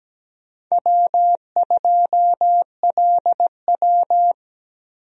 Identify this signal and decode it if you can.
Morse code, heard as a keyed tone, about 17 wpm, 700 Hz. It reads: W2LW
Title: USA Call Signs - 4 Characters - 17wpm